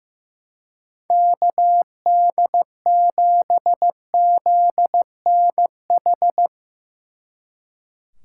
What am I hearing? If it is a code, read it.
Morse code, heard as a keyed tone, about 15 wpm, 700 Hz. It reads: KD7ZNH
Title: Speed Racing - USA 2x3 Call Signs 15wpm